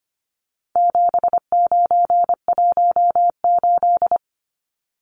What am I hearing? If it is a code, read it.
Morse code, heard as a keyed tone, about 25 wpm, 700 Hz. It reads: 7918
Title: Numbers - 4-Digit 25wpm